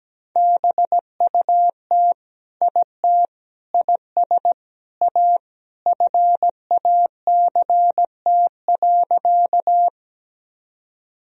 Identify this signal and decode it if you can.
Morse code, heard as a keyed tone, about 17 wpm, 700 Hz. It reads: BUT IT IS A FACT.
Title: Sentences from Top 1000 Words - Review Part 3 - 17wpm